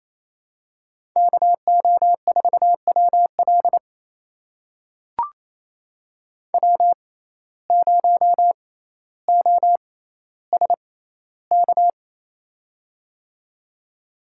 Morse code, heard as a keyed tone, about 28 wpm, spaced as for 14 wpm, 700 Hz. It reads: KO4WL W0OSK
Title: USA Call Signs - 5 Characters - 6x Character Spacing - 28wpm